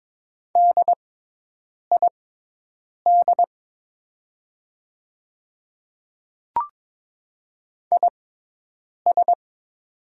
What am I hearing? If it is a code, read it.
Morse code, heard as a keyed tone, about 23 wpm, spaced as for 8 wpm, 700 Hz. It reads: DID IS
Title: 6x Character Spacing - Top 100 Words - 22wpm